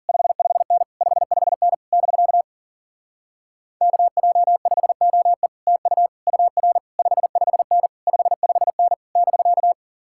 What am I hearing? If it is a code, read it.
Morse code, heard as a keyed tone, about 52 wpm, 700 Hz. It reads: 55N 55N <BK> KJ5OE TU UR 55N 55N <BK>